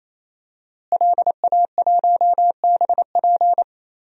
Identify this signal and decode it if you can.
Morse code, heard as a keyed tone, about 28 wpm, 690 Hz. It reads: LA1BP
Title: DXCC-100 Call Signs - 5 Characters - Repeated 3x - 28wpm